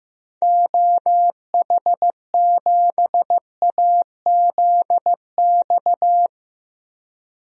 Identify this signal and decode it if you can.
Morse code, heard as a keyed tone, about 15 wpm, 700 Hz. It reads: OH7AZX
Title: DXCC-100 Call Signs - 6 Characters - Repeated 2x - 15wpm